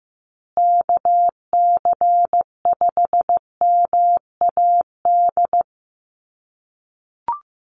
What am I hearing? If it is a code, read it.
Morse code, heard as a keyed tone, about 15 wpm, 700 Hz. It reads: KC5MAD
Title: USA Call Signs Repeated 3x 15wpm